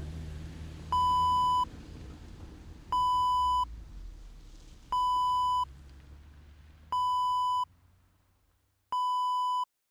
There is no code.